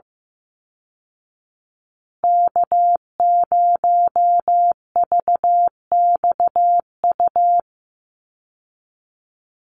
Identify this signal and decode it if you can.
Morse code, heard as a keyed tone, about 15 wpm, 700 Hz. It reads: K0VXU
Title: Call Signs Repeated 3x - 15wpm